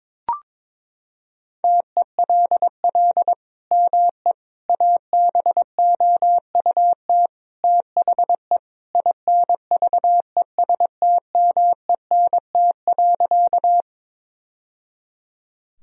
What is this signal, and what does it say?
Morse code, heard as a keyed tone, about 22 wpm, 700 Hz. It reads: TELL ME ABOUT THE INVESTMENT.
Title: Sentences from Top 1000 Words - Review Part 4 - 22wpm